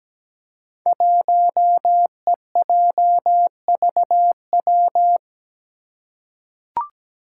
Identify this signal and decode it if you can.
Morse code, heard as a keyed tone, about 17 wpm, 700 Hz. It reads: 1EJVW
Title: Five Challenging Letter-Number Combinations 17wpm